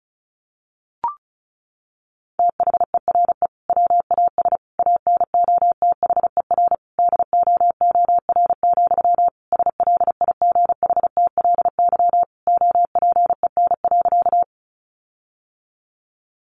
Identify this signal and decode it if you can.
Morse code, heard as a keyed tone, about 34 wpm, 700 Hz. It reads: THERE WAS ANOTHER DOOR, SLIGHTLY OPEN.
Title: Sentences from Top 1300 Words 35wpm